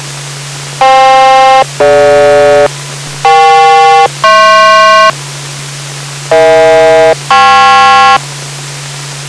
Aircraft SELCAL, heard as two pairs of tones. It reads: FK-DG EK-HP EH-KP